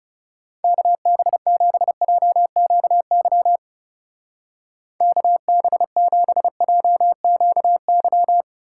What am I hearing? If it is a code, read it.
Morse code, heard as a keyed tone, about 38 wpm, 700 Hz. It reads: KB7JQY KB7JQY